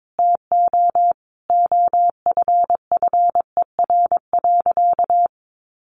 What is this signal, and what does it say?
Morse code, heard as a keyed tone, about 22 wpm, 700 Hz. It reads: TO OFFER.
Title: Sentences from Top 1100 Words 22wpm